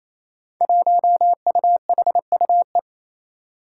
Morse code, heard as a keyed tone, about 28 wpm, 700 Hz. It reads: 1UHUE
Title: Five Challenging Letter-Number Combinations 28wpm